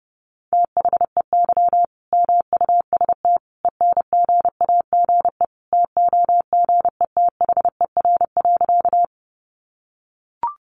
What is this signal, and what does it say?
Morse code, heard as a keyed tone, about 30 wpm, 700 Hz. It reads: THEY MUST ENGAGE TOGETHER.
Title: Sentences from Top 1500 Words 30wpm